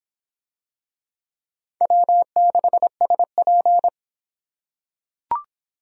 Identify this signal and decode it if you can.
Morse code, heard as a keyed tone, about 26 wpm, 700 Hz. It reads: W6SP